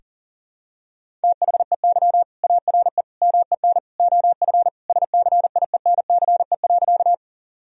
Morse code, heard as a keyed tone, about 40 wpm, 700 Hz. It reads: THEY ARE MEN OF SCIENCE.
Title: Sentences from Top 1000 Words - Review Part 2 - 40wpm